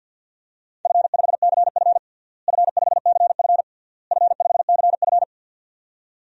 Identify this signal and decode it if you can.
Morse code, heard as a keyed tone, about 51 wpm, 700 Hz. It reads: F5CF F5CF F5CF
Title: DXCC-100 Call Signs - 4 Characters - Repeated 3x - 50wpm